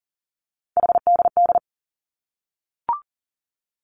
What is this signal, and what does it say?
Morse code, heard as a keyed tone, about 42 wpm, 700 Hz. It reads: HDD
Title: Three Challenging Letter-Number Combinations 40wpm